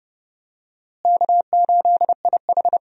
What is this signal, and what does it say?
Morse code, heard as a keyed tone, about 30 wpm, 700 Hz. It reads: K8IH
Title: USA Call Signs - 4 Characters - Repeated 2x - 30wpm